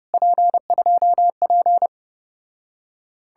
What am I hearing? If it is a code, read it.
Morse code, heard as a keyed tone, about 33 wpm, 700 Hz. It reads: P2P